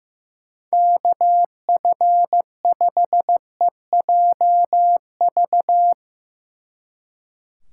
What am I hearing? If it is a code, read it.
Morse code, heard as a keyed tone, about 16 wpm, 700 Hz. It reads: KF5EJV
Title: Speed Racing - USA 2x3 Call Signs 15wpm